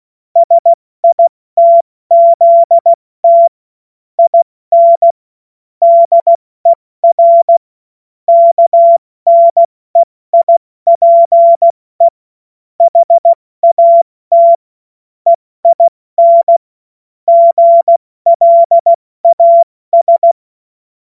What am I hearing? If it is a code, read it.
Morse code, heard as a keyed tone, about 16 wpm, spaced as for 14 wpm, 680 Hz. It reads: SITZT IN DER KNEIPE HAT EIN GLAS
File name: Morsekurs / Witze / Witz 22. Februar 2022 70 F80